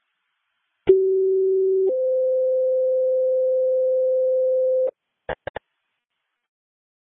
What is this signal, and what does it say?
Two-tone page, heard as a one-second tone, then a three-second tone, about 380 then 520 Hz. It is heard from 0.9 to 4.9 s.